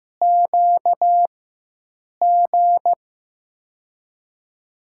Morse code, heard as a keyed tone, about 15 wpm, spaced as for 7 wpm, 700 Hz. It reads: QG
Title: USA Call Signs - 6 Characters - 4x Character Spacing - 15wpm